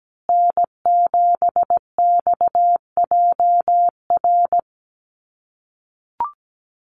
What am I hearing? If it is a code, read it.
Morse code, heard as a keyed tone, about 17 wpm, 700 Hz. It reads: N7XJR
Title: USA Call Signs Repeated 3x 17wpm